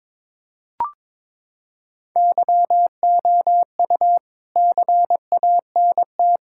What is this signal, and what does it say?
Morse code, heard as a keyed tone, about 21 wpm, 700 Hz. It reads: YOU CANT
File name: practice / 22wpm / Sets of 7 Words - 22wpm